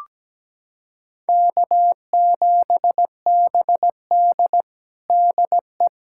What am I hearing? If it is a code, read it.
Morse code, heard as a keyed tone, about 17 wpm, 700 Hz. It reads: K7BD DE
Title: USA Callsign DE Callsign - 4 Characters - 17wpm